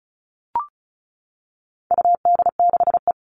Morse code, heard as a keyed tone, about 34 wpm, 700 Hz. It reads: UD6E
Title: All Cut Numbers - 4-Digit 35wpm